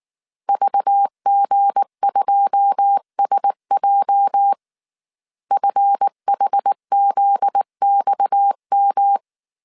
Morse code, heard as a keyed tone, about 19 wpm, 800 Hz. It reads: VG2SJ FHZXM